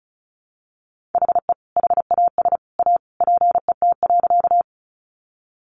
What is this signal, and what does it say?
Morse code, heard as a keyed tone, about 34 wpm, 700 Hz. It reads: HE HAS A PET.
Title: Easy Phrases 35wpm